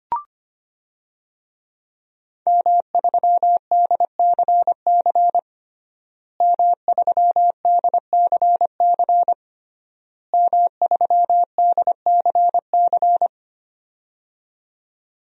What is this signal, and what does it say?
Morse code, heard as a keyed tone, about 25 wpm, 700 Hz. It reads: M3DCC M3DCC M3DCC
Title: DXCC-100 Call Signs - 5 Characters - Repeated 3x - 25wpm